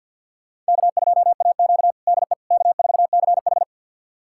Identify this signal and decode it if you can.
Morse code, heard as a keyed tone, about 52 wpm, 700 Hz. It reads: K2AQ DE K4CH
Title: USA Callsign DE Callsign - 4 Characters - 50wpm